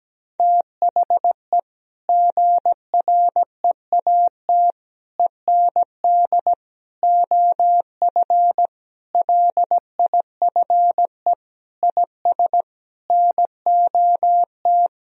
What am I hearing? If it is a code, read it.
Morse code, heard as a keyed tone, about 17 wpm, 700 Hz. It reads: THE GREAT END OF LIFE IS NOT
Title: Sentences from Top 1000 Words - Review Part 4 - 17wpm